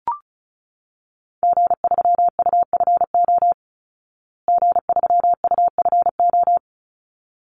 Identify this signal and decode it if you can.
Morse code, heard as a keyed tone, about 34 wpm, 700 Hz. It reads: G3UFO G3UFO
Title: Call Signs Repeated 3x - 35wpm